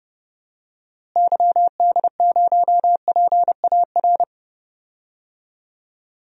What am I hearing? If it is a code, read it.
Morse code, heard as a keyed tone, about 30 wpm, 700 Hz. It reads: YD0PAR
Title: DXCC-100 Call Signs - 6 Characters - Repeated 2x - 30wpm